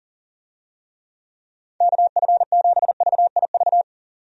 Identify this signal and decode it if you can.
Morse code, heard as a keyed tone, about 40 wpm, 690 Hz. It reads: KF7VIV
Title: USA Call Signs - 6 Characters - 40wpm